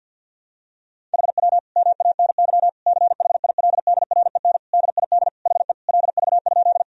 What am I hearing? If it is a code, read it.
Morse code, heard as a keyed tone, about 50 wpm, 700 Hz. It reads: HW MANY CHILDREN DID HE LV?
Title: Sentences from Top 400 Words and Abbreviations 50wpm